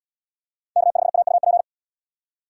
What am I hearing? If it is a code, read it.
Morse code, heard as a keyed tone, about 68 wpm, 700 Hz. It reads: K5AUP